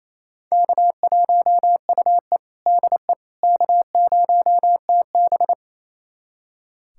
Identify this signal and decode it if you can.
Morse code, heard as a keyed tone, about 28 wpm, 700 Hz. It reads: K1UE DE K0TB
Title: USA Callsign DE Callsign - 4 Characters - 28wpm